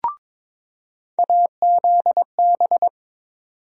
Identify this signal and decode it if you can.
Morse code, heard as a keyed tone, about 22 wpm, 700 Hz. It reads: AZB
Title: USA Call Sign Postfixes - 3 Letters 22wpm